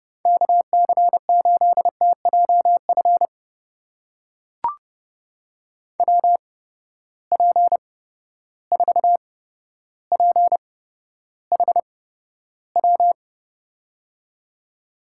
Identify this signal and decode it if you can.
Morse code, heard as a keyed tone, about 32 wpm, spaced as for 13 wpm, 700 Hz. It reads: KC8TJF WP4PHW
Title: USA Call Signs - 6 Characters - 8x Character Spacing - 30wpm